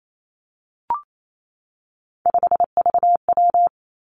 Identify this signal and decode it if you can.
Morse code, heard as a keyed tone, about 28 wpm, 700 Hz. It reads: 5VW
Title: Three Challenging Letter-Number Combinations 28wpm